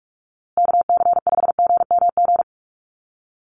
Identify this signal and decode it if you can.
Morse code, heard as a keyed tone, about 44 wpm, 690 Hz. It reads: KC5GMG
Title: USA Call Signs Repeated 3x 45wpm